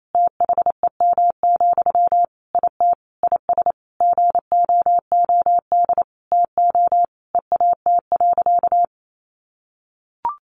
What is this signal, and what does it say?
Morse code, heard as a keyed tone, about 28 wpm, 700 Hz. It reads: THEM, IT IS GOOD TO EAT.
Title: Sentences from Top 800 Words 28wpm